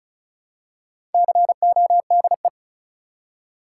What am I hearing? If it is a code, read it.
Morse code, heard as a keyed tone, about 36 wpm, 700 Hz. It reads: CODE
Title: Four Letter Words - 35wpm